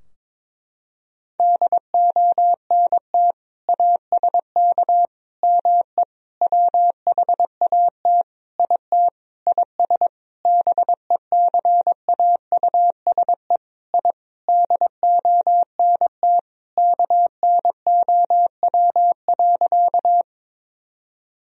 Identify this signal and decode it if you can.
Morse code, heard as a keyed tone, about 22 wpm, 700 Hz. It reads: DONT ASK ME WHAT IT IS BECAUSE I DONT KNOW.
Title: Sentences from Top 1000 Words - Review Part 3 - 22wpm